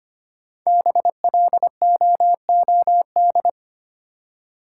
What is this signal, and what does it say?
Morse code, heard as a keyed tone, about 25 wpm, 700 Hz. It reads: BLOOD
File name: 5 Letter Words - First 3 Letters Given - 25wpm